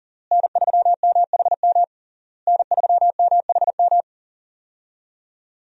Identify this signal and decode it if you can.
Morse code, heard as a keyed tone, about 40 wpm, 700 Hz. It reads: N3MHM N3MHM
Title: USA Call Signs - 5 Characters - Repeated 3x - 40wpm